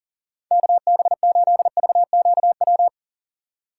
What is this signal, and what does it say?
Morse code, heard as a keyed tone, about 40 wpm, 700 Hz. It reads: KB8VQW
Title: USA Call Signs - 6 Characters - 2x Character Spacing - 40wpm